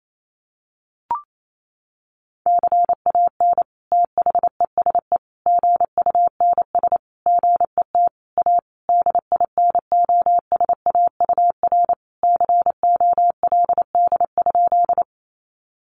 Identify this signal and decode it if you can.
Morse code, heard as a keyed tone, about 28 wpm, 700 Hz. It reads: CAN THESE GUNS GET A DINOSAUR COLD?
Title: A Sound Of Thunder 28wpm